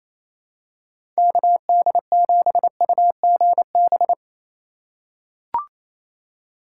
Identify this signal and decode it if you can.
Morse code, heard as a keyed tone, about 28 wpm, 700 Hz. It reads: KD7UGB
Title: USA Call Signs - 4x Character Spacing - 28wpm